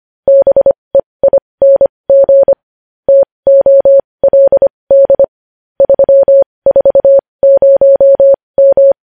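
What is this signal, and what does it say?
Morse code, heard as a keyed tone, about 25 wpm, spaced as for 20 wpm, 550 Hz. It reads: BEING TOLD 340M
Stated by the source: News headlines sent in Morse code at 20.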